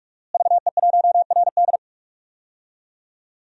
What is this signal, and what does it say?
Morse code, heard as a keyed tone, about 48 wpm, 700 Hz. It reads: VE1RD
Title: DXCC-100 Call Signs - 5 Characters - Repeated 3x - 45wpm